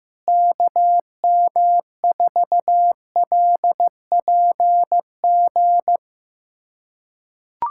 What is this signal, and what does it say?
Morse code, heard as a keyed tone, about 15 wpm, 700 Hz. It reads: KM4LPG
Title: USA Callsign DE Callsign - 6 Characters - 15wpm